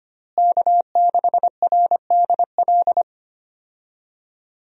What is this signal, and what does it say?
Morse code, heard as a keyed tone, about 25 wpm, 700 Hz. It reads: K6RDL
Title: USA Call Signs - 5 Characters - Repeated 3x - 25wpm